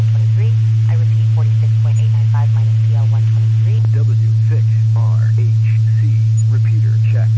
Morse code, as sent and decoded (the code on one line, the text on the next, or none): none